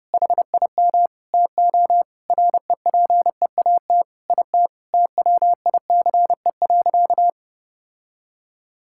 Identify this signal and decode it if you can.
Morse code, heard as a keyed tone, about 29 wpm, 700 Hz. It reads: HIM TO REPEAT IT TWICE.
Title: Sentences from Top 1600 Words 30wpm